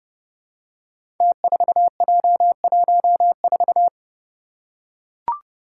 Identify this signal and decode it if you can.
Morse code, heard as a keyed tone, about 30 wpm, 700 Hz. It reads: T4J14
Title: Five Challenging Letter-Number Combinations 30wpm